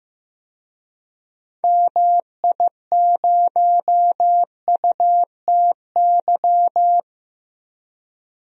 Morse code, heard as a keyed tone, about 16 wpm, 700 Hz. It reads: MI0UTY
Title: DXCC-100 Call Signs - 6 Characters - 15wpm